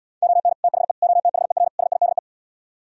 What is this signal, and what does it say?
Morse code, heard as a keyed tone, about 77 wpm, 700 Hz. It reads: ON THE OTHER SIDE